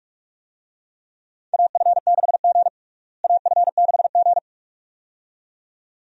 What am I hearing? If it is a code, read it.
Morse code, heard as a keyed tone, about 47 wpm, 700 Hz. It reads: AF6G AF6G
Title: USA Call Signs - 4 Characters - Repeated 2x - 45wpm